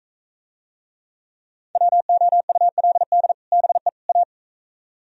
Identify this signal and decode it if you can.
Morse code, heard as a keyed tone, about 45 wpm, 700 Hz. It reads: WOULD BE A